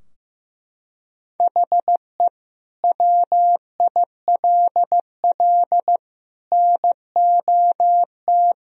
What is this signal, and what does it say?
Morse code, heard as a keyed tone, about 15 wpm, 700 Hz. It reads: HE WILL NOT
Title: Sentences from Top 1000 Words - Review Part 4 - 15wpm